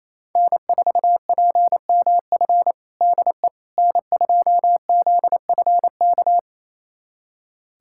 Morse code, heard as a keyed tone, about 27 wpm, 690 Hz. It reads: N4PMF DE N2ZFK
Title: USA Callsign DE Callsign - 5 Characters - 28wpm